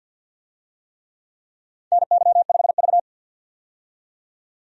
Morse code, heard as a keyed tone, about 51 wpm, 700 Hz. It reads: NY5V
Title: USA Call Signs - 4 Characters - 50wpm